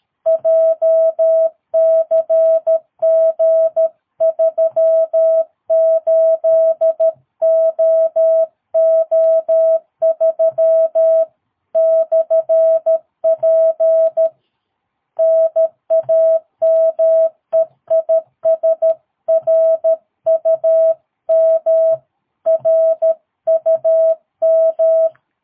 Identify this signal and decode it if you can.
Morse code, heard as a keyed tone, about 13 wpm, 650 Hz. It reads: JCG38OO3/P NAME IS RUM RUM
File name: jh8cbh / 2022cw